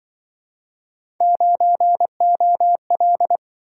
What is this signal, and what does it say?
Morse code, heard as a keyed tone, about 26 wpm, 700 Hz. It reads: 9OL